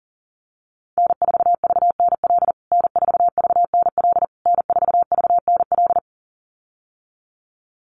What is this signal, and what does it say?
Morse code, heard as a keyed tone, about 40 wpm, 700 Hz. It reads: N4VNL N4VNL N4VNL
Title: USA Call Signs Repeated 3x 40wpm